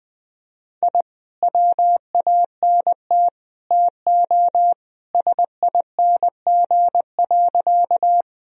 Morse code, heard as a keyed tone, about 20 wpm, 700 Hz. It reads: I WANT TO SING.
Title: Sentences from Top 1000 Words - Review Part 2 - 20wpm